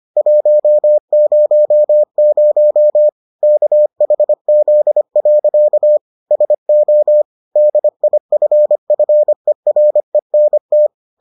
Morse code, heard as a keyed tone, about 25 wpm, 600 Hz. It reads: 100 KHZ. SO DIFFERENT